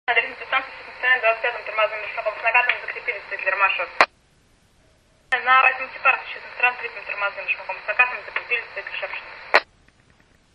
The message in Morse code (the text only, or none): none